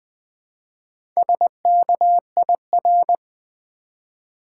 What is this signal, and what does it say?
Morse code, heard as a keyed tone, about 20 wpm, 700 Hz. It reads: SKIR
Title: Build A Word - 1 Letter At A Time - 5-Letter Words - 20wpm